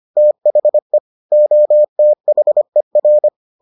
Morse code, heard as a keyed tone, about 25 wpm, 600 Hz. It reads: THE OTHER